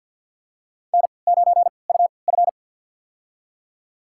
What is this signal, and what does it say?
Morse code, heard as a keyed tone, about 49 wpm, spaced as for 28 wpm, 700 Hz. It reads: N9UF
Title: USA Call Signs - 3x Character Spacing - 50wpm